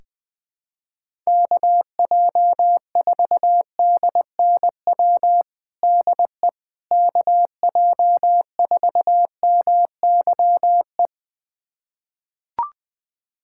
Morse code, heard as a keyed tone, about 20 wpm, 700 Hz. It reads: KJ4DNW DE KJ4MYE
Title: USA Callsign DE Callsign - 6 Characters - 20wpm